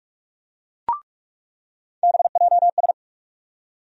Morse code, heard as a keyed tone, about 46 wpm, 700 Hz. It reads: BJS
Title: USA Call Sign Postfixes - 3 Letters 45wpm